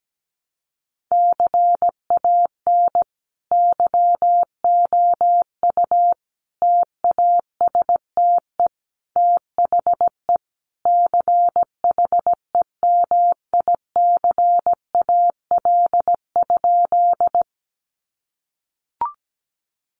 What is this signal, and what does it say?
Morse code, heard as a keyed tone, about 17 wpm, 700 Hz. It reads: CAN YOU TASTE THE CHEMICAL?
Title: Sentences from Top 1200 Words 17wpm